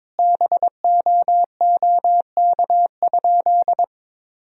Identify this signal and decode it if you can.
Morse code, heard as a keyed tone, about 22 wpm, 700 Hz. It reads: BOOK?
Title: Sentences from Top 1000 Words - Review Part 1 - 22wpm